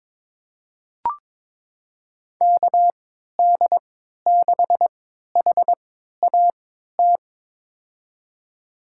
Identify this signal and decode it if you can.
Morse code, heard as a keyed tone, about 21 wpm, spaced as for 12 wpm, 700 Hz. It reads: KD6HAT
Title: USA Call Signs - 6 Characters - 3x Character Spacing - 22wpm